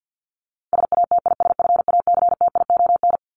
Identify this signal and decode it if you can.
Morse code, heard as a keyed tone, about 48 wpm, 700 Hz. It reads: SATISFACTION
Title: Top 1751-2000 Words 50wpm